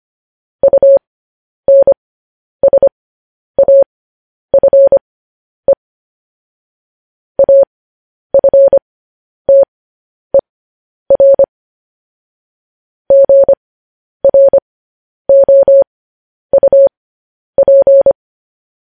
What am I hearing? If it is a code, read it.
Morse code, heard as a keyed tone, about 25 wpm, spaced as for 10 wpm, 550 Hz. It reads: UNSAFE AFTER GROUP
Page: News Headlines in Morse Code at 10 WPM